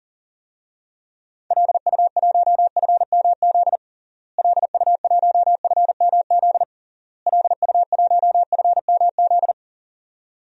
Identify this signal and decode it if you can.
Morse code, heard as a keyed tone, about 40 wpm, 700 Hz. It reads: LU1FMZ LU1FMZ LU1FMZ
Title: DXCC-100 Call Signs - 6 Characters - Repeated 3x - 40wpm